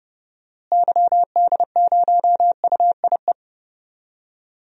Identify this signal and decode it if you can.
Morse code, heard as a keyed tone, about 29 wpm, 700 Hz. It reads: YD0UIE
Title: DXCC-100 Call Signs - 6 Characters - Repeated 2x - 30wpm